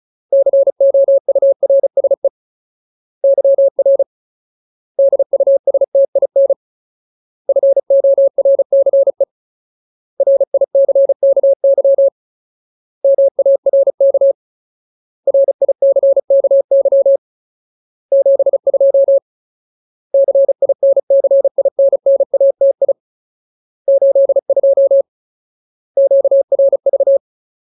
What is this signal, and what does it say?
Morse code, heard as a keyed tone, about 34 wpm, 550 Hz. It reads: COURSE YR DUSTIN FORCE RICKY MARK RICKY 72 CINCINNATI 82 QRV
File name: Ultimate Word List / 35WPM Ultimate